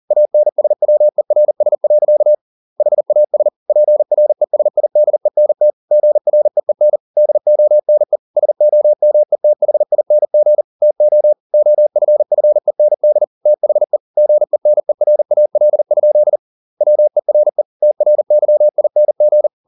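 Morse code, heard as a keyed tone, about 40 wpm, 600 Hz. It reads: ANSWERS. HAS PRESIDENT GREEN DONE SOMETHING TO OFFEND THE GENERAL? WERE TRYING
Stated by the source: Morse code at 40 wpm, practice text, CW med Gnister.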